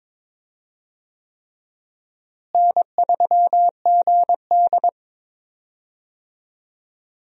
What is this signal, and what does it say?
Morse code, heard as a keyed tone, about 22 wpm, 700 Hz. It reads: N3GD
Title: USA Call Signs - 4 Characters - Repeated 3x - 22wpm